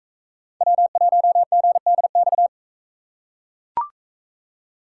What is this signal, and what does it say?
Morse code, heard as a keyed tone, about 41 wpm, 700 Hz. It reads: W1GDX